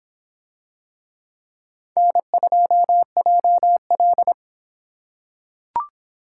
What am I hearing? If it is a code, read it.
Morse code, heard as a keyed tone, about 26 wpm, 700 Hz. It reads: N2JL